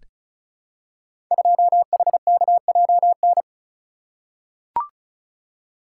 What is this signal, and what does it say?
Morse code, heard as a keyed tone, about 36 wpm, 700 Hz. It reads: 2HKJN
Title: Code Groups - 5 Letter:Numbers - 35wpm